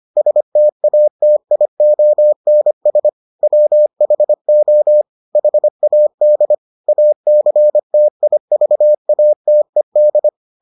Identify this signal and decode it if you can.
Morse code, heard as a keyed tone, about 25 wpm, 600 Hz. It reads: STATIONS WHO HAD ACTIVATED